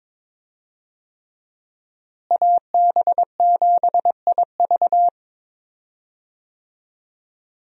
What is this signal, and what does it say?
Morse code, heard as a keyed tone, about 21 wpm, 700 Hz. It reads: AB7IV
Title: USA Call Signs - 5 Characters - Repeated 3x - 22wpm